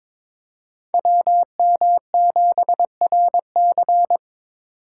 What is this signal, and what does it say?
Morse code, heard as a keyed tone, about 26 wpm, 700 Hz. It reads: WM7RC
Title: Speed Racing - USA 2x2 Call Signs 22wpm